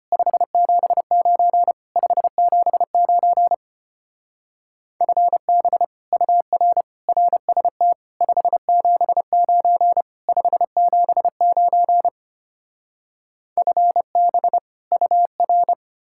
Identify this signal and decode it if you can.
Morse code, heard as a keyed tone, about 33 wpm, 700 Hz. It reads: 579 579 FB UR RST 579 579 FB UR